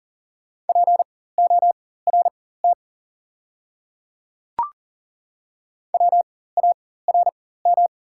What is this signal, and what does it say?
Morse code, heard as a keyed tone, about 40 wpm, spaced as for 19 wpm, 690 Hz. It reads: PORT WARM
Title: 4x Character Spacing - Top 500 Words - 40wpm